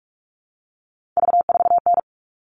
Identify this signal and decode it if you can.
Morse code, heard as a keyed tone, about 44 wpm, 700 Hz. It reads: V4N
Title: All Cut Numbers - 3-Digit 45wpm